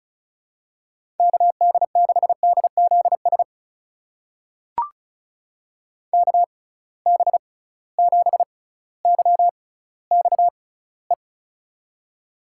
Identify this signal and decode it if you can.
Morse code, heard as a keyed tone, about 34 wpm, spaced as for 18 wpm, 700 Hz. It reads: KD6DZS KB7YXE
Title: USA Call Signs - 6 Characters - 6x Character Spacing - 35wpm